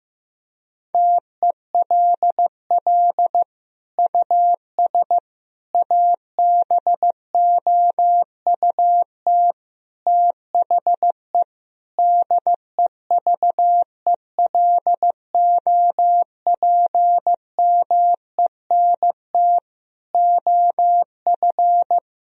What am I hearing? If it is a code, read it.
Morse code, heard as a keyed tone, about 15 wpm, 700 Hz. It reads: TELL US ABOUT THE DEVELOPMENT OF
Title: Sentences from Top 1000 Words - Review Part 4 - 15wpm